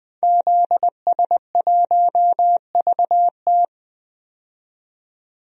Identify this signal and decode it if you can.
Morse code, heard as a keyed tone, about 20 wpm, 700 Hz. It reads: ZS1VT
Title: DXCC-100 Call Signs - 5 Characters - 20wpm